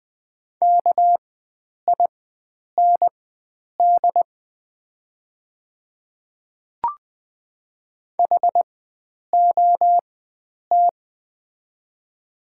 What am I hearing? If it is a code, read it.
Morse code, heard as a keyed tone, about 20 wpm, spaced as for 9 wpm, 690 Hz. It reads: KIND HOT
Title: 4x Character Spacing - Top 300 Words - 20wpm